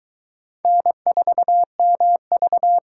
Morse code, heard as a keyed tone, about 22 wpm, 700 Hz. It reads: N4MV